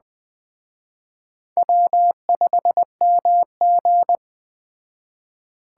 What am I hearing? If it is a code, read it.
Morse code, heard as a keyed tone, about 20 wpm, 700 Hz. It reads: W5MG
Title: Speed Racing - USA 1x2 Call Signs 17wpm